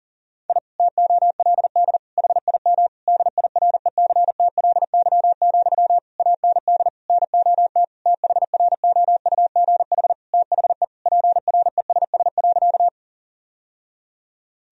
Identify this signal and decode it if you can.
Morse code, heard as a keyed tone, about 37 wpm, 700 Hz. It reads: I TOLD HIM DIRECTLY, AND NOT THROUGH THE PRESS.
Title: Sentences from Top 1000 Words - Review Part 1 - 40wpm